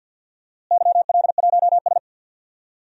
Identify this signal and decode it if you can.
Morse code, heard as a keyed tone, about 51 wpm, 700 Hz. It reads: YL1S
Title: DXCC-100 Call Signs - 4 Characters - Repeated 2x - 50wpm